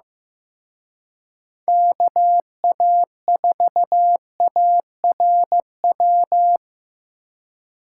Morse code, heard as a keyed tone, about 18 wpm, 700 Hz. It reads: KA4ARW
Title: Speed Racing - USA 2x3 Call Signs 15wpm